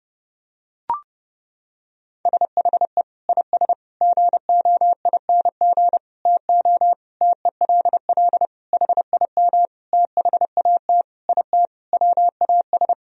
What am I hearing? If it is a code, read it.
Morse code, heard as a keyed tone, about 29 wpm, 700 Hz. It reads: SHE IS GOING TO TELL HIM THAT IT WAS
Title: Sentences from Top 1000 Words - Review Part 1 - 30wpm